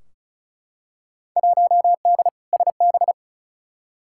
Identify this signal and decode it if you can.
Morse code, heard as a keyed tone, about 36 wpm, 700 Hz. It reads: 1D SB
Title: Field Day Exchanges - 35wpm